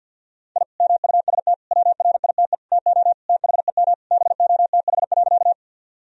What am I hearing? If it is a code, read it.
Morse code, heard as a keyed tone, about 50 wpm, 700 Hz. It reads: I MUST WRITE TO THEM BOTH.
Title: Sentences from Top 1000 Words - Review Part 1 - 50wpm